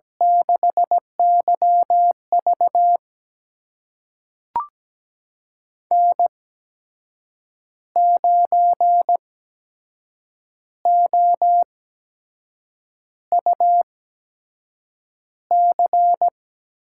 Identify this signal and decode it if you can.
Morse code, heard as a keyed tone, about 17 wpm, spaced as for 6 wpm, 700 Hz. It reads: 6YV N9OUC
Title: USA Call Signs - 8x Character Spacing - 17wpm